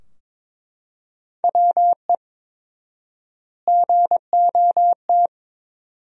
Morse code, heard as a keyed tone, about 22 wpm, 700 Hz. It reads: WE GOT
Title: 4x Word-Spacing - Sentences from Top 300 Words - 22wpm